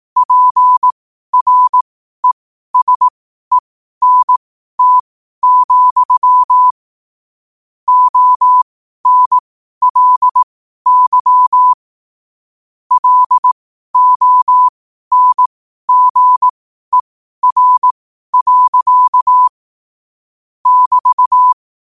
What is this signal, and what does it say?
Morse code, heard as a keyed tone, about 18 wpm, spaced as for 12 wpm, 1000 Hz. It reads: PRESENT, ONLY LONGER. <BT>
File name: qotdfiles / 12wpm